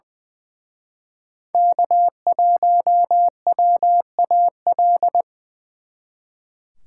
Morse code, heard as a keyed tone, about 24 wpm, 700 Hz. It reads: K1WAL
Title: Speed Racing - USA Call Signs 20wpm